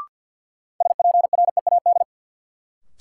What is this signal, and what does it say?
Morse code, heard as a keyed tone, about 51 wpm, 700 Hz. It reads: SPREAD
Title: Sentences from Top 1200 Words - New Word Drill - 50wpm